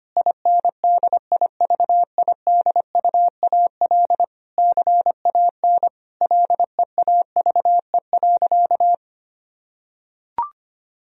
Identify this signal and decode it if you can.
Morse code, heard as a keyed tone, about 25 wpm, 700 Hz. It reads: INDIVIDUAL CAN LEAVE.
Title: Sentences from Top 1000 Words - Review Part 4 - 25wpm